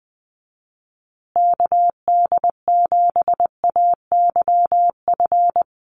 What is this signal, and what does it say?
Morse code, heard as a keyed tone, about 20 wpm, 700 Hz. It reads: KD7AYF
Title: USA Call Signs Repeated 3x 20wpm